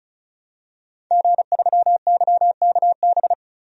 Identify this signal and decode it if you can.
Morse code, heard as a keyed tone, about 36 wpm, 700 Hz. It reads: G3YKB
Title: DXCC-100 Call Signs - 5 Characters - 35wpm